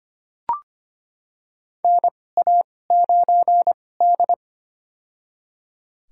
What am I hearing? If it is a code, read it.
Morse code, heard as a keyed tone, about 25 wpm, spaced as for 18 wpm, 700 Hz. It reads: NA9D
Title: USA Call Signs - 4 Characters - 2x Character Spacing - 25wpm